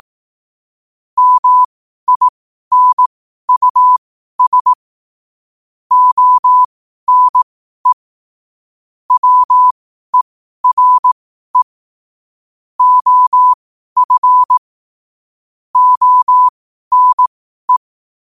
Morse code, heard as a keyed tone, about 18 wpm, spaced as for 12 wpm, 1000 Hz. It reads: MINUS ONE WERE OF ONE
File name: qotdfiles / 12wpm